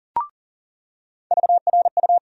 Morse code, heard as a keyed tone, about 40 wpm, 700 Hz. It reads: VRU
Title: USA Call Sign Postfixes - 3 Letters 40wpm